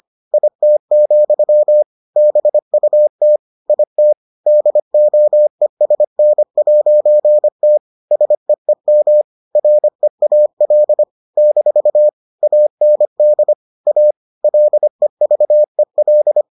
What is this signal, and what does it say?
Morse code, heard as a keyed tone, about 25 wpm, 600 Hz. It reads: IT, BUT IT DOESN'T SEEM REAL - AND A LEVEL